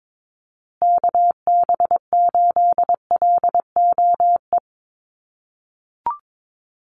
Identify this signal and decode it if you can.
Morse code, heard as a keyed tone, about 22 wpm, 700 Hz. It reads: KB8LOE
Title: USA Call Signs Repeated 2x 22wpm